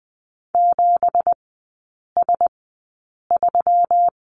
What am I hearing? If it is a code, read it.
Morse code, heard as a keyed tone, about 20 wpm, spaced as for 8 wpm, 700 Hz. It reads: 7S3
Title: 172 Lesson 28e - Number 9 - Three Characters 20wpm